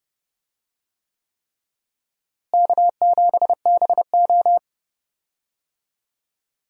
Morse code, heard as a keyed tone, about 30 wpm, 700 Hz. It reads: K7BO
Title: USA Call Signs - 4 Characters - Repeated 3x - 30wpm